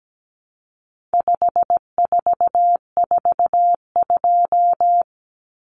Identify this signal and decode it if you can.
Morse code, heard as a keyed tone, about 17 wpm, 700 Hz. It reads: 5442
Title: Numbers - 4-Digit 17wpm